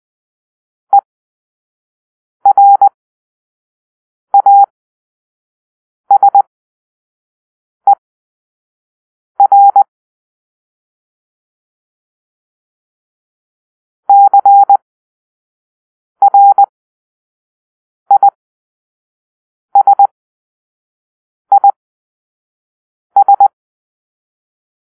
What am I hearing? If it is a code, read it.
Morse code, heard as a keyed tone, about 20 wpm, spaced as for 5 wpm, 800 Hz. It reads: ERASER CRISIS
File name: arrl / Grp-2 Drill-2